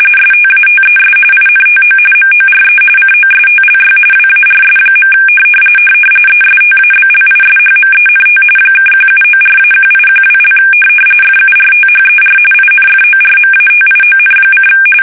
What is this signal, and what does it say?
RTTY, about 45.45 baud. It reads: STATES INTO MORE COSTLY WARS. ESTABLISHMENT (1794-1812) FURTHER INFORMATION:
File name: fsk850 / usnavy1